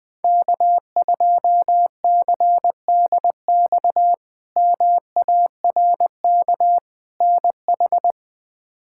Morse code, heard as a keyed tone, about 20 wpm, 700 Hz. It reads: K2CDX MARK NH
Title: K1USN SST QSO Practice - 20wpm